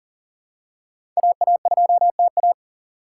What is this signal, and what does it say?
Morse code, heard as a keyed tone, about 37 wpm, 700 Hz. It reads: AA2TA
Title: USA Call Signs - 5 Characters - 4x Character Spacing - 40wpm